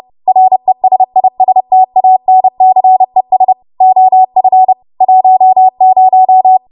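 Morse code, heard as a keyed tone, about 30 wpm, 750 Hz. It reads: RESISTANCES OF 10
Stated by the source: W1AW Web Code Practice Files, 30 WPM run from ARRL; text from April 2023 QST.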